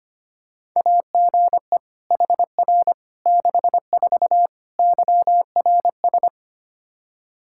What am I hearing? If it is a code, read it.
Morse code, heard as a keyed tone, about 25 wpm, 700 Hz. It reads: AGE HR 64 YRS
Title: K1USN SST QSO Practice 25wpm